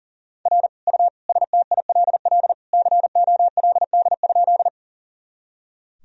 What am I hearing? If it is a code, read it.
Morse code, heard as a keyed tone, about 40 wpm, 700 Hz. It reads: R U STILL COLD?
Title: Sentences from Top 400 Words and Abbreviations 40wpm